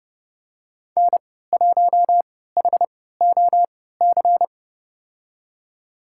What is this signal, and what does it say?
Morse code, heard as a keyed tone, about 30 wpm, spaced as for 17 wpm, 700 Hz. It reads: N1HOC
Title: USA Call Signs - 5 Characters - 3x Character Spacing - 30wpm